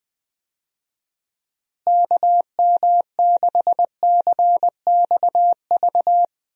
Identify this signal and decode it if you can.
Morse code, heard as a keyed tone, about 20 wpm, 700 Hz. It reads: KM6CXV
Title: USA Call Signs - 6 Characters - Repeated 3x - 20wpm